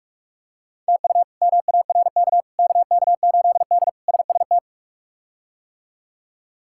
Morse code, heard as a keyed tone, about 44 wpm, 700 Hz. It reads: TU MARK KK8D SST
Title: K1USN SST QSO Practice - 45wpm